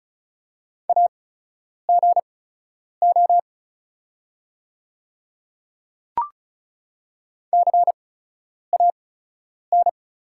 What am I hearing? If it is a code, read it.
Morse code, heard as a keyed tone, about 36 wpm, spaced as for 10 wpm, 700 Hz. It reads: AGO CAN
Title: 8x Character Spacing - Top 500 Words - 35wpm